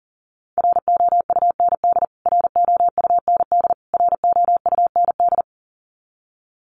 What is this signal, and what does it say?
Morse code, heard as a keyed tone, about 40 wpm, 700 Hz. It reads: ROUND ROUND ROUND
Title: Top 200 Words Repeated 40wpm